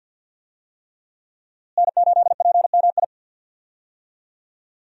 Morse code, heard as a keyed tone, about 53 wpm, 700 Hz. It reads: N8PMI
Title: USA Call Signs - 5 Characters - Repeated 3x - 50wpm